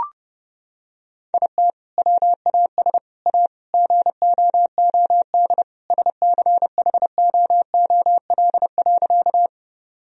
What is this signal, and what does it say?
Morse code, heard as a keyed tone, about 32 wpm, 700 Hz. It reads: IT WAS A GOOD SCHOOL.
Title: Sentences from Top 1000 Words - Review Part 1 - 30wpm